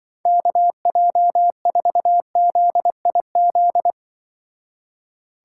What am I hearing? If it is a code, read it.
Morse code, heard as a keyed tone, about 25 wpm, 700 Hz. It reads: KJ4ZIZ